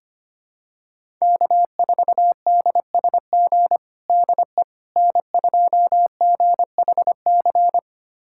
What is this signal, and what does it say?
Morse code, heard as a keyed tone, about 25 wpm, 700 Hz. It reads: K4DSG DE N2GHC
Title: USA Callsign DE Callsign - 5 Characters - 25wpm